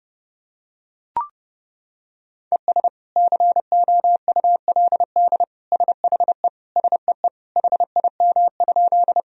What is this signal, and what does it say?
Morse code, heard as a keyed tone, about 32 wpm, 700 Hz. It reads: ES COULD SHE SEE HIM?
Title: Sentences from Top 100 Words and Abbreviations 30wpm